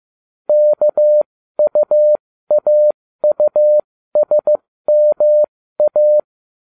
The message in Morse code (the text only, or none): KUAUSMA